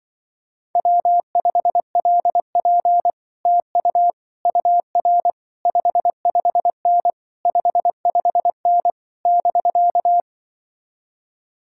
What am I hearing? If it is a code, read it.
Morse code, heard as a keyed tone, about 24 wpm, 700 Hz. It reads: W5LP TU UR 55N 55N <BK>
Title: Speed-Racing Common QSO Phrases Encore - 20wpm